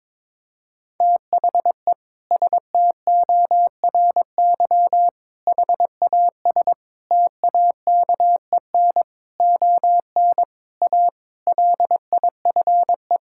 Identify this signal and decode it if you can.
Morse code, heard as a keyed tone, about 22 wpm, 700 Hz. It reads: THE STORY HAS TAKEN ON A LIFE
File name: practice / 22wpm / Sentences with 8-Word Phrases - 22wpm